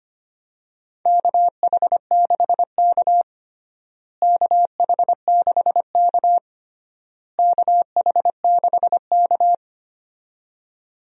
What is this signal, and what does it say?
Morse code, heard as a keyed tone, about 25 wpm, 700 Hz. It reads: KH6K KH6K KH6K
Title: DXCC-100 Call Signs - 4 Characters - Repeated 3x - 25wpm